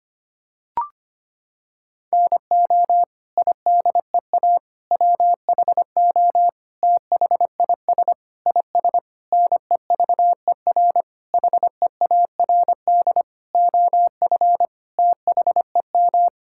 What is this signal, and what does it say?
Morse code, heard as a keyed tone, about 25 wpm, 700 Hz. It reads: NO IDEA WHO THIS IS NEVER HEARD OF THEM
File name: Sets of 9 Words - 25wpm